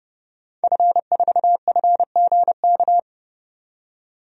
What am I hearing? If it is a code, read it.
Morse code, heard as a keyed tone, about 29 wpm, 700 Hz. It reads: F4FGK
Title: DXCC-100 Call Signs - 5 Characters - 30wpm